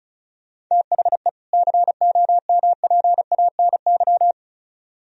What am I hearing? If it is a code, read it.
Morse code, heard as a keyed tone, about 43 wpm, 700 Hz. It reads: THE COMPANY
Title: Speed-Racing - Sets of 2 Words 35wpm